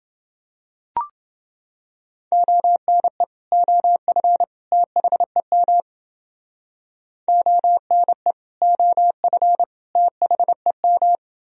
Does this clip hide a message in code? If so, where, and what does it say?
Yes: Morse code from 2.3 to 11.2 s, heard as a keyed tone, about 29 wpm, 700 Hz. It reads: ONE OF THEM ONE OF THEM